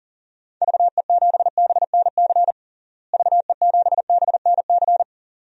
Morse code, heard as a keyed tone, about 40 wpm, 700 Hz. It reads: VE7BNC VE7BNC
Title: DXCC-100 Call Signs - 6 Characters - Repeated 2x - 40wpm